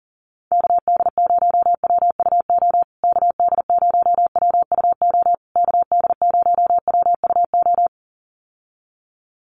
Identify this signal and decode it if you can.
Morse code, heard as a keyed tone, about 40 wpm, 700 Hz. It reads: KD0WUO KD0WUO KD0WUO
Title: USA Call Signs Repeated 3x 40wpm